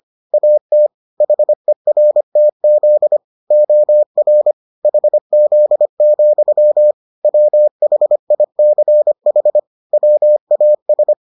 Morse code, heard as a keyed tone, about 25 wpm, 600 Hz. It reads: AT HERTZ OR HZ, WHICH WAS